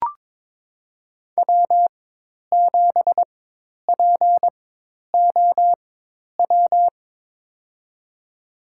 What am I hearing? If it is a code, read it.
Morse code, heard as a keyed tone, about 22 wpm, spaced as for 10 wpm, 700 Hz. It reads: W7POW
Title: USA Call Signs - 5 Characters - 4x Character Spacing - 22wpm